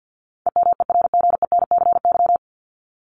Morse code, heard as a keyed tone, about 50 wpm, 700 Hz. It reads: EMERGENCY
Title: Top 1501-1750 Words 50wpm